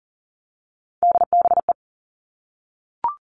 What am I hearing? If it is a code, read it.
Morse code, heard as a keyed tone, about 40 wpm, 700 Hz. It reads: DBE
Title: All Cut Numbers - 3-Digit 40wpm